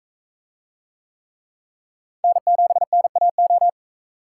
Morse code, heard as a keyed tone, about 44 wpm, 700 Hz. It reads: N7NAO